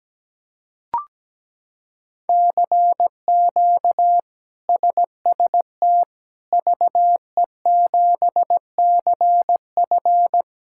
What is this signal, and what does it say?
Morse code, heard as a keyed tone, about 17 wpm, 700 Hz. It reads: CQ SST VE7CF
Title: K1USN SST QSO Practice - 17wpm